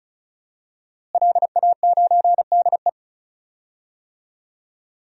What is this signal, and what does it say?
Morse code, heard as a keyed tone, about 34 wpm, 700 Hz. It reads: LA9DE
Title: DXCC-100 Call Signs - 5 Characters - Repeated 2x - 35wpm